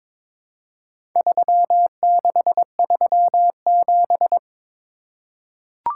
Morse code, heard as a keyed tone, about 22 wpm, 700 Hz. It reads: 3637
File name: practice / 22wpm / Four Letter-Number Combinations - 22wpm